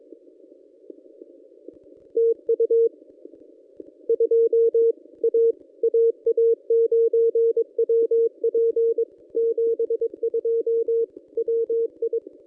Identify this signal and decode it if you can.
Morse code, heard as a keyed tone, about 22 wpm, 460 Hz. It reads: TU 2 A AA9WP 72 WI
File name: contests / ss-cw-k5tr-op-n5rz -1100